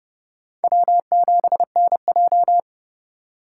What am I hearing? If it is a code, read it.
Morse code, heard as a keyed tone, about 32 wpm, 700 Hz. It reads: W7NJ
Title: USA Call Signs - 2x Character Spacing - 30wpm